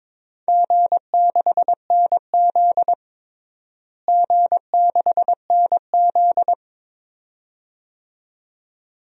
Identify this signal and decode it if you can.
Morse code, heard as a keyed tone, about 21 wpm, 700 Hz. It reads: G6NZ G6NZ
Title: DXCC-100 Call Signs - 4 Characters - Repeated 2x - 22wpm